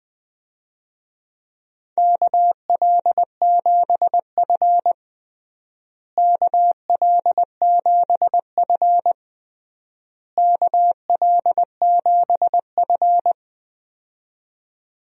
Morse code, heard as a keyed tone, about 20 wpm, 700 Hz. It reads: KL7F KL7F KL7F
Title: DXCC-100 Call Signs - 4 Characters - Repeated 3x - 20wpm